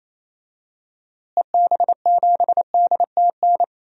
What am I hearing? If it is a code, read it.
Morse code, heard as a keyed tone, about 28 wpm, 700 Hz. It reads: EB7DTN
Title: DXCC-100 Call Signs - 6 Characters - Repeated 2x - 28wpm